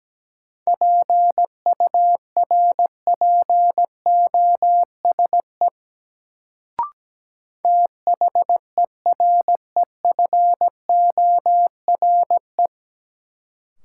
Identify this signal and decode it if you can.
Morse code, heard as a keyed tone, about 17 wpm, 700 Hz. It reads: PURPOSE THEREFORE
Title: Sentences from Top 800 Words - New Word Drill - 17wpm